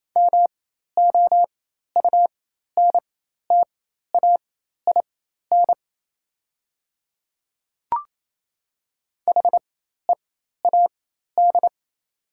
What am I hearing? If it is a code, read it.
Morse code, heard as a keyed tone, about 28 wpm, spaced as for 13 wpm, 700 Hz. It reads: MOUNTAIN HEAD
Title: 4x Character Spacing - Top 300 Words - 28wpm